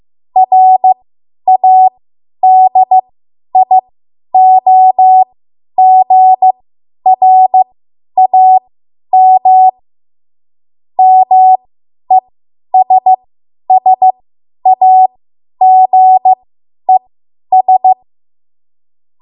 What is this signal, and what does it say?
Morse code, heard as a keyed tone, about 15 wpm, spaced as for 10 wpm, 750 Hz. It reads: RADIOGRAM MESSAGES
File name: Morse / 10WPM